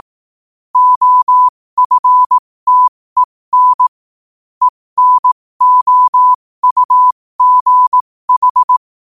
Morse code, heard as a keyed tone, about 18 wpm, spaced as for 15 wpm, 1000 Hz. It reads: OFTEN ENOUGH
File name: qotdfiles / 15wpm